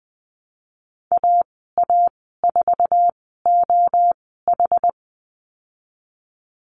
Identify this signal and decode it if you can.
Morse code, heard as a keyed tone, about 20 wpm, spaced as for 15 wpm, 700 Hz. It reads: AA4OH
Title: 062 Lesson 11g - Letter h - Call Signs 20wpm